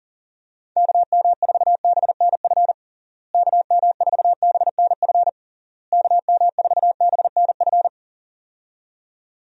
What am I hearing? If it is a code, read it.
Morse code, heard as a keyed tone, about 40 wpm, 700 Hz. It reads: KM4BNF KM4BNF KM4BNF
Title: USA Call Signs - 6 Characters - Repeated 3x - 40wpm